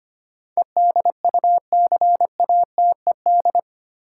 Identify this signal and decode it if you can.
Morse code, heard as a keyed tone, about 25 wpm, 700 Hz. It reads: EDUCATED
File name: 8 Letter Words - First 3 Letters Given - 25wpm